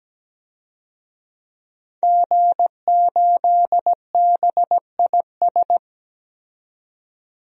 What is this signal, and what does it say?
Morse code, heard as a keyed tone, about 17 wpm, 700 Hz. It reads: G8BIS
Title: DXCC-100 Call Signs - 5 Characters - 17wpm